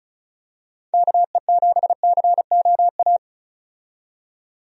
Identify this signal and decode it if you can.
Morse code, heard as a keyed tone, about 34 wpm, 700 Hz. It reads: KE7COA
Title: USA Call Signs - 6 Characters - 35wpm